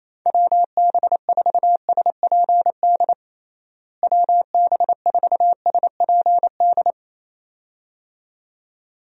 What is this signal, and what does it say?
Morse code, heard as a keyed tone, about 28 wpm, 700 Hz. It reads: WB4SPD WB4SPD
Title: USA Call Signs - 6 Characters - Repeated 2x - 28wpm